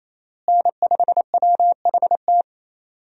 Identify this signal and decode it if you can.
Morse code, heard as a keyed tone, about 27 wpm, 700 Hz. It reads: N5WHT
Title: USA Call Signs - 5 Characters - Repeated 2x - 28wpm